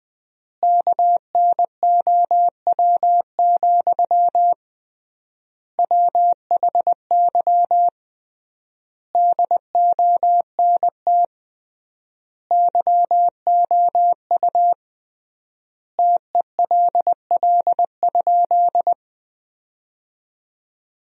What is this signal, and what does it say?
Morse code, heard as a keyed tone, about 20 wpm, 700 Hz. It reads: KNOW, WHY DONT YOU TELL?
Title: 3x Word-Spacing - Sentences from Top 300 Words - 20wpm